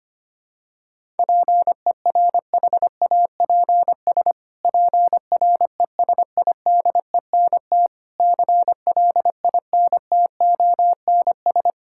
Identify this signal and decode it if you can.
Morse code, heard as a keyed tone, about 25 wpm, 700 Hz. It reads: PERHAPS PRESIDENT CLINTONS
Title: Pangram Challenge v2 - 25wpm